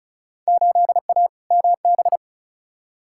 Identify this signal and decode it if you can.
Morse code, heard as a keyed tone, about 34 wpm, 700 Hz. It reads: 8A MB
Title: Field Day Exchanges Encore - 35wpm